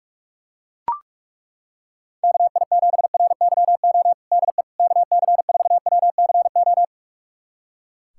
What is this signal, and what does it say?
Morse code, heard as a keyed tone, about 41 wpm, 700 Hz. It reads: KI7RYO DE KC4WCO
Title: USA Callsign DE Callsign - 6 Characters - 45wpm